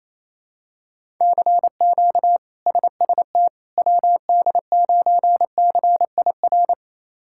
Morse code, heard as a keyed tone, about 28 wpm, 700 Hz. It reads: CQ SST WD9CIR
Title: K1USN SST QSO Practice - 28wpm